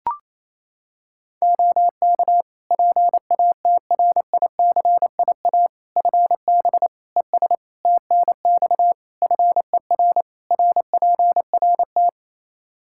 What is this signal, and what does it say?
Morse code, heard as a keyed tone, about 28 wpm, 700 Hz. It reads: OK PATRICIA FB ES TNX FER RPRT
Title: Practice QSO - Exchange 2 - Info - 28wpm